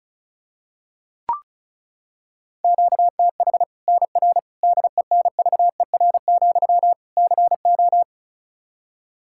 Morse code, heard as a keyed tone, about 36 wpm, 700 Hz. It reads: QTH NR DENVER, CO
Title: Practice QSO - Exchange 1 - Beginning - 35wpm